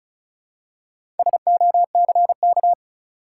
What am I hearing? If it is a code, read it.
Morse code, heard as a keyed tone, about 36 wpm, 700 Hz. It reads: SOCK
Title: Four Letter Words - 35wpm